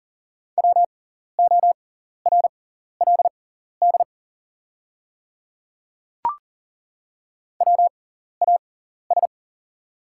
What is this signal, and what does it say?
Morse code, heard as a keyed tone, about 40 wpm, spaced as for 14 wpm, 700 Hz. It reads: WORLD WAS
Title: 6x Character Spacing - Top 200 Words - 40wpm